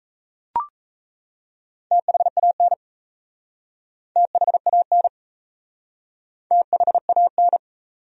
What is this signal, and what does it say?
Morse code, heard as a keyed tone, about 38 wpm, 700 Hz. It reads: THAN THAN THAN